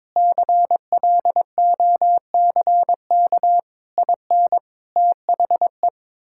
Morse code, heard as a keyed tone, about 21 wpm, 700 Hz. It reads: CLOCK IN THE
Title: Pangram Challenge v2 - 22wpm